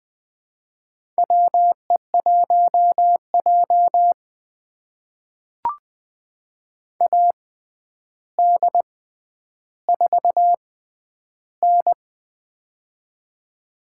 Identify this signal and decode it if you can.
Morse code, heard as a keyed tone, about 20 wpm, spaced as for 10 wpm, 700 Hz. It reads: WE1J AD4N
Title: USA Call Signs - 4 Characters - 6x Character Spacing - 20wpm